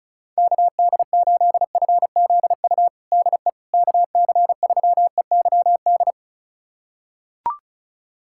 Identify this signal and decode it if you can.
Morse code, heard as a keyed tone, about 36 wpm, 700 Hz. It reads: KD8FZU DE KC3EYD
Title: USA Callsign DE Callsign 35wpm